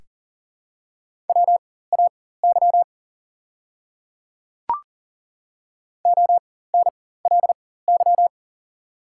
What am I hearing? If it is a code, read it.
Morse code, heard as a keyed tone, about 40 wpm, spaced as for 19 wpm, 700 Hz. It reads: WAY ONLY
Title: 4x Character Spacing - Top 200 Words - 40wpm